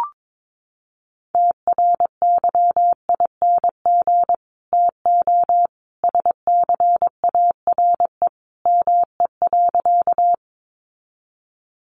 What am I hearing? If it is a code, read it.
Morse code, heard as a keyed tone, about 22 wpm, 700 Hz. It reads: TRYING TO SCARE ME.
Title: A Sound Of Thunder 22wpm